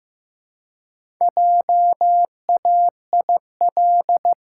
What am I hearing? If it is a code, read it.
Morse code, heard as a keyed tone, about 15 wpm, 700 Hz. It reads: JAIL
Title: Pangram Challenge v2 - 15wpm